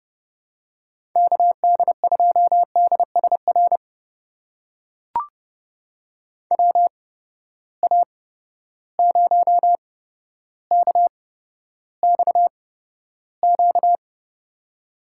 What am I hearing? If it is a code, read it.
Morse code, heard as a keyed tone, about 29 wpm, spaced as for 13 wpm, 700 Hz. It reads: KD2DSR WA0KXQ
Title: USA Call Signs - 6 Characters - 8x Character Spacing - 30wpm